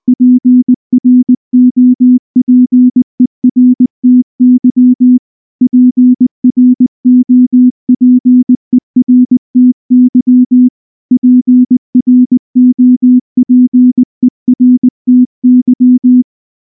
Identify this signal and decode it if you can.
Morse code, heard as a keyed tone, about 19 wpm, 250 Hz. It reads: PROPERTY PROPERTY PROPERTY
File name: morse / morse-training-20WPM-1